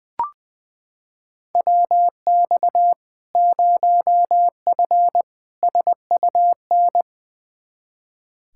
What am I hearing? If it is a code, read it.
Morse code, heard as a keyed tone, about 20 wpm, 700 Hz. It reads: WX 0F SUN
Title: Practice QSO - Exchange 2 - Info - 20wpm